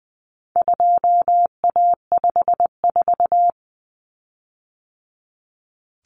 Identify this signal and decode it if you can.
Morse code, heard as a keyed tone, about 20 wpm, 700 Hz. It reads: 2A54
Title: Common Cut Numbers - 4-Digit 20wpm